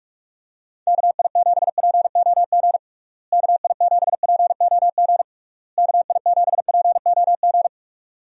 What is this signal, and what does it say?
Morse code, heard as a keyed tone, about 41 wpm, 700 Hz. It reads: KI7POG KI7POG KI7POG
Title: USA Call Signs - 6 Characters - Repeated 3x - 45wpm